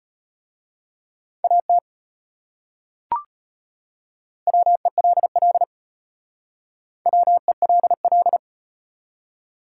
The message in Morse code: AT WELL WELL

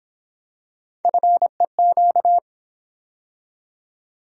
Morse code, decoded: FEQ

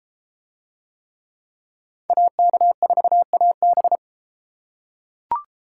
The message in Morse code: AK4AB

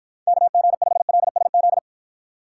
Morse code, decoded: KG5LSZ

